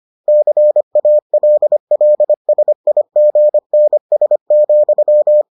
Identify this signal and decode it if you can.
Morse code, heard as a keyed tone, about 25 wpm, 600 Hz. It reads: CALLSIGNS,